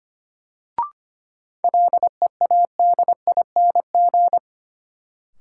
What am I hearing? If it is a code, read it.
Morse code, heard as a keyed tone, about 25 wpm, 700 Hz. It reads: LEADING
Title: Sentences from Top 1100 Words - New Word Drill - 25wpm